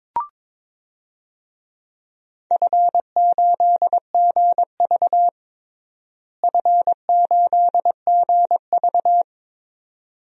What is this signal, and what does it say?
Morse code, heard as a keyed tone, about 22 wpm, 700 Hz. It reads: F8GV F8GV
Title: DXCC-100 Call Signs - 4 Characters - Repeated 2x - 22wpm